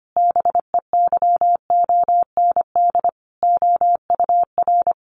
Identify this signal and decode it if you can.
Morse code, heard as a keyed tone, about 25 wpm, 700 Hz. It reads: BEYOND OUR
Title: Sentences from Top 1500 Words 25wpm